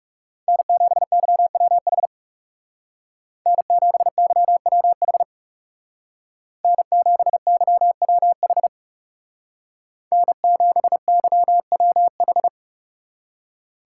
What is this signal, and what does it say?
Morse code, heard as a keyed tone, about 39 wpm, 700 Hz. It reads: N7YWH N7YWH N7YWH N7YWH